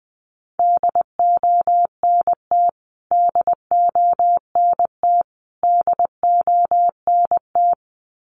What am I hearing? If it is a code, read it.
Morse code, heard as a keyed tone, about 20 wpm, 700 Hz. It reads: DONT DONT DONT
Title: Top 500 Words Repeated 20wpm